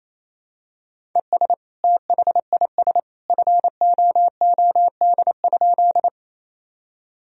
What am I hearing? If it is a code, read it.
Morse code, heard as a keyed tone, about 28 wpm, 700 Hz. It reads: ES THIS FOOD?
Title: Sentences from Top 300 Words and Abbreviations 28wpm